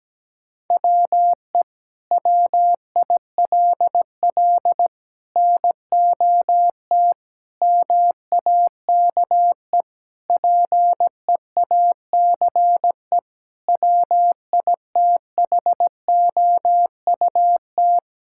Morse code, heard as a keyed tone, about 17 wpm, 700 Hz. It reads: WE WILL NOT MAKE PEACE WITHOUT
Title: Sentences from Top 1600 Words 17wpm